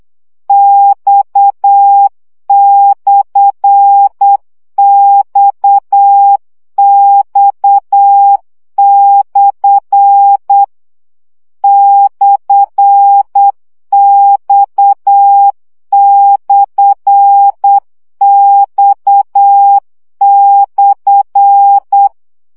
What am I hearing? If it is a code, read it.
Morse code, heard as a keyed tone, about 8 wpm, 800 Hz. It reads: X/XX/ /X/X/
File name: CW gyakorlas x per 7 baud